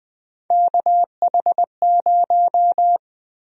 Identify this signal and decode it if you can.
Morse code, heard as a keyed tone, about 21 wpm, 700 Hz. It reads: KH0
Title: USA Call Sign Prefixes - Repeated 2x - 20wpm